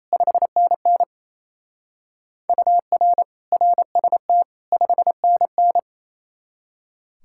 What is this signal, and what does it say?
Morse code, heard as a keyed tone, about 32 wpm, 700 Hz. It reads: 5NN UR RST 5NN
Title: Speed-Racing Common QSO Phrases - 28wpm